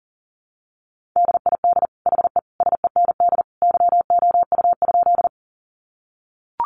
Morse code, heard as a keyed tone, about 40 wpm, 700 Hz. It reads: DID HE SEND YOU?
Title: Sentences from Top 1500 Words 40wpm